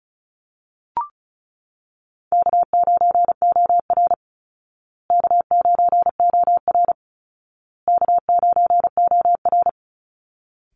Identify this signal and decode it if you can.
Morse code, heard as a keyed tone, about 34 wpm, 700 Hz. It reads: K9OR K9OR K9OR
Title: Easy Call Signs Repeated 3x - 35wpm